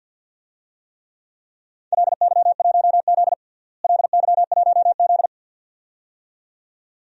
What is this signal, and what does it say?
Morse code, heard as a keyed tone, about 47 wpm, 700 Hz. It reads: LY1Z LY1Z
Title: DXCC-100 Call Signs - 4 Characters - Repeated 2x - 50wpm